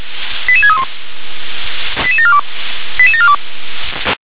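Five-tone selective call (ZVEI), heard as quick runs of five tones, tones of about 70 ms. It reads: 88631 88742 88742
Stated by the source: ZVEI five-tone call received by SDR.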